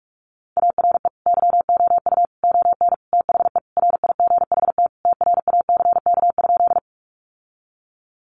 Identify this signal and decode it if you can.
Morse code, heard as a keyed tone, about 44 wpm, 700 Hz. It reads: ARE YOU ON THE RIGHT TRACK?
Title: Sentences from Top 1100 Words 45wpm